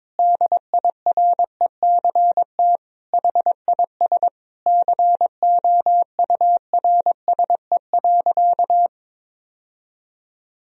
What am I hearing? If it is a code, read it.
Morse code, heard as a keyed tone, about 22 wpm, 700 Hz. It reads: DIRECT HIS COURSE.
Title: Sentences from Top 1000 Words - Review Part 2 - 22wpm